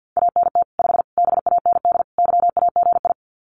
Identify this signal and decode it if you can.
Morse code, heard as a keyed tone, about 50 wpm, 700 Hz. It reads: ANT 5 BAND YAGI
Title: Common QSO Phrases 50wpm